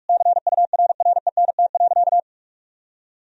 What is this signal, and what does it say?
Morse code, heard as a keyed tone, about 41 wpm, 700 Hz. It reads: CURRENT.
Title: Sentences from Top 1000 Words - Review Part 4 - 45wpm